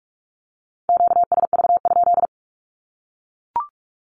Morse code, heard as a keyed tone, about 44 wpm, 700 Hz. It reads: QSV?
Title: Q-Codes Repeated 3x 45wpm